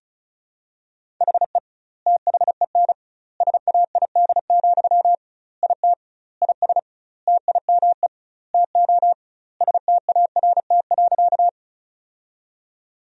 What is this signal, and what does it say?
Morse code, heard as a keyed tone, about 34 wpm, 700 Hz. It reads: HE THEN SAID, IT IS TIME TO START.
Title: 2x Word-Spacing - Sentences from Top 300 Words - 35wpm